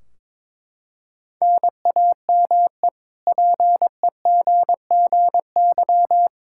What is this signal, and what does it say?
Morse code, heard as a keyed tone, about 21 wpm, 700 Hz. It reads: NAME PEGGY
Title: K1USN SST QSO Practice 22wpm